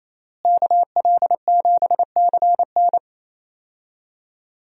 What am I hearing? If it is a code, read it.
Morse code, heard as a keyed tone, about 28 wpm, 700 Hz. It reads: KL7CN
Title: USA Call Signs - 5 Characters - Repeated 2x - 28wpm